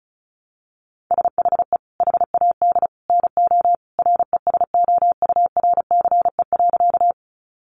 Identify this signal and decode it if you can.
Morse code, heard as a keyed tone, about 34 wpm, 700 Hz. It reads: SHE HAD NO RESOURCE.
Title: Sentences from Top 1300 Words 35wpm